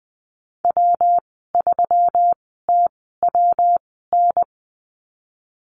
Morse code, heard as a keyed tone, about 20 wpm, spaced as for 15 wpm, 700 Hz. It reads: W3TWN
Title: 154 Lesson 25g - Letter p - Call Signs 20wpm